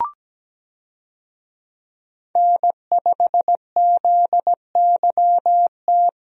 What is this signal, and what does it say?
Morse code, heard as a keyed tone, about 17 wpm, 700 Hz. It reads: N5ZYT
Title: USA Call Signs - 5 Characters - Repeated 2x - 17wpm